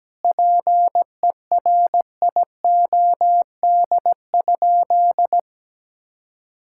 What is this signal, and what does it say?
Morse code, heard as a keyed tone, about 17 wpm, 700 Hz. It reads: PERIOD?
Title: Sentences from Top 1000 Words - Review Part 5 - 17wpm